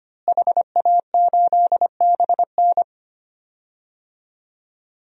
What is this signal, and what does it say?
Morse code, heard as a keyed tone, about 25 wpm, 700 Hz. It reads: HA8BN
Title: DXCC-100 Call Signs - 5 Characters - Repeated 2x - 25wpm